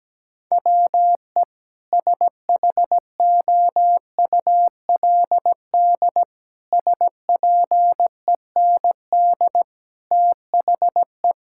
Morse code, heard as a keyed tone, about 17 wpm, 700 Hz. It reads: WE SHOULD SPEND THE
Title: Sentences from Top 1000 Words - Review Part 5 - 17wpm